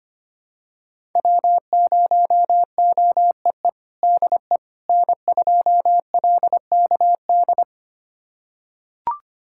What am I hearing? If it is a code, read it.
Morse code, heard as a keyed tone, about 25 wpm, 700 Hz. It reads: W0OEE DE N2LKD
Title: USA Callsign DE Callsign - 5 Characters - 25wpm